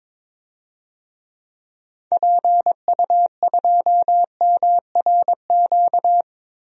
Morse code, heard as a keyed tone, about 22 wpm, 700 Hz. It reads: PU2MRQ
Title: DXCC-100 Call Signs - 6 Characters - 22wpm